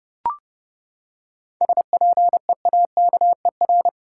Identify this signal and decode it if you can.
Morse code, heard as a keyed tone, about 30 wpm, 700 Hz. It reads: SPEAKER
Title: Seven Letter Words - 30wpm